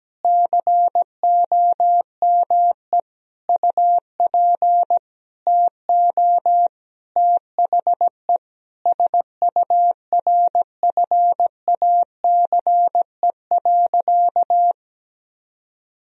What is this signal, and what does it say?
Morse code, heard as a keyed tone, about 17 wpm, 700 Hz. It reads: COME UP TO THE SURFACE.
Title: Sentences from Top 1000 Words - Review Part 3 - 17wpm